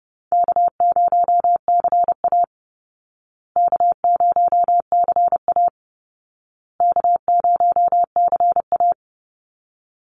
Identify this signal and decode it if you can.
Morse code, heard as a keyed tone, about 30 wpm, 700 Hz. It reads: K0CA K0CA K0CA
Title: Call Signs Repeated 2x - 30wpm